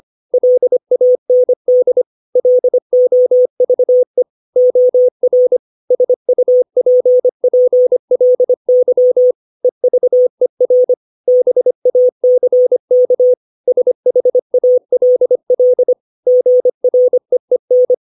Morse code, heard as a keyed tone, about 25 wpm, 500 Hz. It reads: LAND LOVE OR SUPPLY EVER BACK SHALL GREEN
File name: 632 words / 632 Words 25WPM